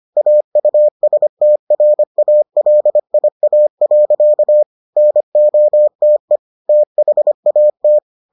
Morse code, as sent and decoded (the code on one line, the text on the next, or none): .- ..- ... - .-. .- .-.. .. .- .-.-.- / -. --- - . / - .... .- -
AUSTRALIA. NOTE THAT